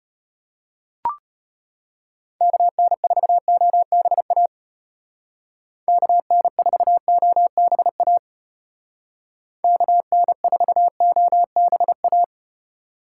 Morse code, heard as a keyed tone, about 34 wpm, 700 Hz. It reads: KN4OBA KN4OBA KN4OBA